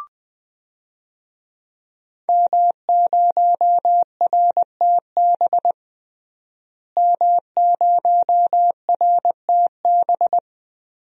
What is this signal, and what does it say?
Morse code, heard as a keyed tone, about 20 wpm, 700 Hz. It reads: M0RTB M0RTB
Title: DXCC-100 Call Signs - 5 Characters - Repeated 2x - 20wpm